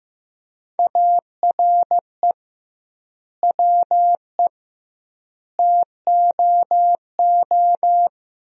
Morse code, heard as a keyed tone, about 15 wpm, 700 Hz. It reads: ARE WE TOO
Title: 2x Word-Spacing - Sentences from Top 300 Words - 15wpm